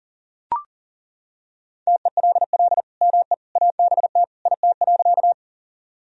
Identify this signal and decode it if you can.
Morse code, heard as a keyed tone, about 37 wpm, 700 Hz. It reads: TELL ME ABT IT.
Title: Sentences from Top 200 Words and Abbreviations 40wpm